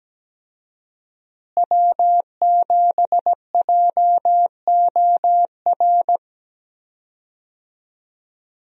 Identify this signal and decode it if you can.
Morse code, heard as a keyed tone, about 17 wpm, 700 Hz. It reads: W7JOR
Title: USA Call Signs - 5 Characters - Repeated 2x - 17wpm